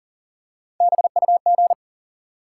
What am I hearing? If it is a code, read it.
Morse code, heard as a keyed tone, about 42 wpm, 700 Hz. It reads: BUG
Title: Three Letter Words - 40wpm